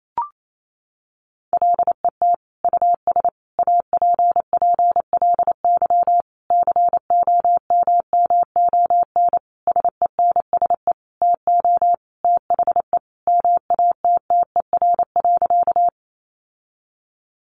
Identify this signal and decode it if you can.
Morse code, heard as a keyed tone, about 28 wpm, 700 Hz. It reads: LET US APPLY COMMON SENSE TO THE MATTER.
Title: Sentences from Top 1200 Words 28wpm